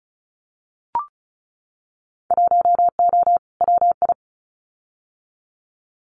Morse code, heard as a keyed tone, about 36 wpm, 700 Hz. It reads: 1O WI
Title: Winter Field Day Exchanges 35wpm